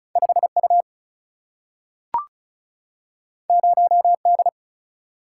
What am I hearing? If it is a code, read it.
Morse code, heard as a keyed tone, about 36 wpm, 700 Hz. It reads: 5U 0D
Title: Single-Letter Single-Number 35wpm